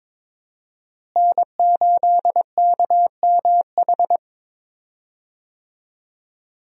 Morse code, heard as a keyed tone, about 22 wpm, 700 Hz. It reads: N8KMH
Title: USA Call Signs - 5 Characters - Repeated 3x - 22wpm